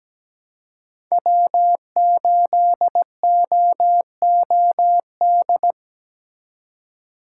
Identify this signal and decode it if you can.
Morse code, heard as a keyed tone, about 17 wpm, 700 Hz. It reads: W8OOD
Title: USA Call Signs - 5 Characters - 4x Character Spacing - 17wpm